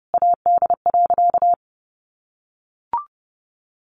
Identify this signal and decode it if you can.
Morse code, heard as a keyed tone, about 30 wpm, 700 Hz. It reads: AD.
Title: Sentences from Top 1300 Words 30wpm